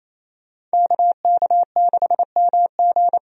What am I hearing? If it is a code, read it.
Morse code, heard as a keyed tone, about 27 wpm, 700 Hz. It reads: KK6MG
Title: USA Call Signs - 5 Characters - 28wpm